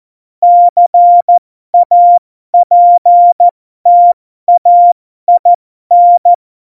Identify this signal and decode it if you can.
Morse code, heard as a keyed tone, about 14 wpm, spaced as for 12 wpm, 700 Hz. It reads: CAPTAIN
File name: Morsekurs / Witze